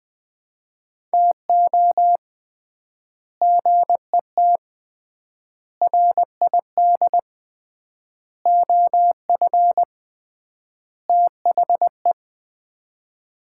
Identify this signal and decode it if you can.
Morse code, heard as a keyed tone, about 21 wpm, 700 Hz. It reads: TO GET RID OF THE
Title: 3x Word-Spacing - Sets of 5 Words - 20wpm